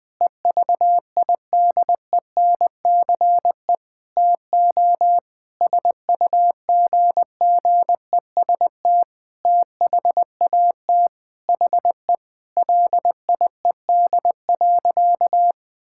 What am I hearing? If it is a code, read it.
Morse code, heard as a keyed tone, about 20 wpm, 700 Hz. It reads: EVIDENCE TO SUGGEST THAT HE LIED.
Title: Sentences with 7-Word Phrases - Spoken Beginning - 20wpm